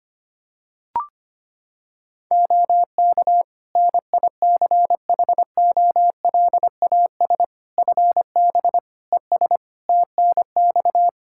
Morse code, heard as a keyed tone, about 25 wpm, 700 Hz. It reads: OK NICHOLAS FB ES TNX
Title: Practice QSO - Exchange 2 - Info - 25wpm